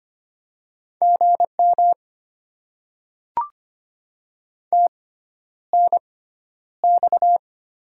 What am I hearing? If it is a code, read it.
Morse code, heard as a keyed tone, about 25 wpm, spaced as for 11 wpm, 700 Hz. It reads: GM TNX
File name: Top 20 QSO Elements - 6x Character Spacing - 25wpm